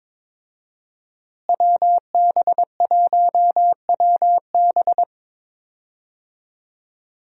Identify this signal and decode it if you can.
Morse code, heard as a keyed tone, about 22 wpm, 700 Hz. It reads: WB1WB
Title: USA Call Signs - 5 Characters - Repeated 2x - 22wpm